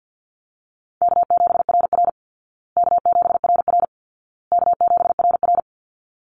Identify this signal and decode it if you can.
Morse code, heard as a keyed tone, about 49 wpm, 700 Hz. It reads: K7RR K7RR K7RR
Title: Easy Call Signs Repeated 3x - 50wpm